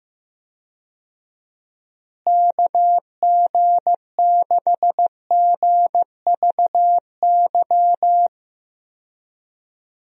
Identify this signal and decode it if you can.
Morse code, heard as a keyed tone, about 16 wpm, 700 Hz. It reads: KG6GVY
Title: USA Call Signs - 6 Characters - Repeated 3x - 15wpm